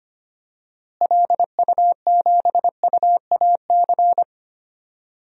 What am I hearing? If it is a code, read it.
Morse code, heard as a keyed tone, about 25 wpm, 700 Hz. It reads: LU7UAC
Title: DXCC-100 Call Signs - 6 Characters - Repeated 3x - 25wpm